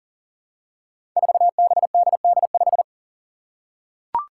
Morse code, heard as a keyed tone, about 38 wpm, 700 Hz. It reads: 4BDD5
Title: Five Challenging Letter-Number Combinations 40wpm